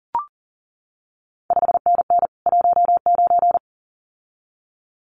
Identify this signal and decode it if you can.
Morse code, heard as a keyed tone, about 40 wpm, 700 Hz. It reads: 5NN 19
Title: CQ WW DX Contest Exchange Encore - 40wpm